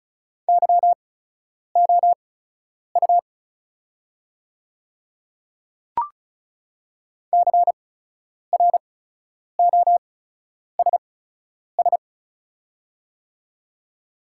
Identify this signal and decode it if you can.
Morse code, heard as a keyed tone, about 36 wpm, spaced as for 10 wpm, 690 Hz. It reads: YOU CROSS
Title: 8x Character Spacing - Top 300 Words - 35wpm